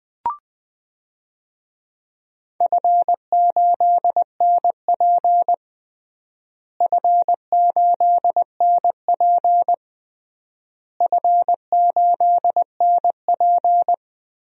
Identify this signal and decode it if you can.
Morse code, heard as a keyed tone, about 20 wpm, 700 Hz. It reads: F8NP F8NP F8NP
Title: DXCC-100 Call Signs - 4 Characters - Repeated 3x - 20wpm